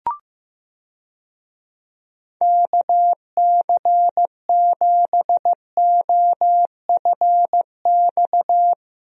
Morse code, heard as a keyed tone, about 15 wpm, 700 Hz. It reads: KC7OFX
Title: USA Call Signs - 6 Characters - Repeated 3x - 15wpm